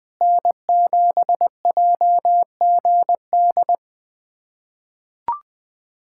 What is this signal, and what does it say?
Morse code, heard as a keyed tone, about 20 wpm, 700 Hz. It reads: N7JGD
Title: USA Callsign DE Callsign - 5 Characters - 20wpm